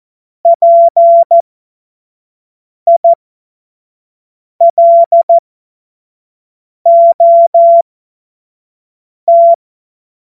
Morse code, heard as a keyed tone, about 14 wpm, spaced as for 5 wpm, 680 Hz. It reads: PILOT
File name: Morsekurs / Witze / Witz 30. Mai 2023 25 F70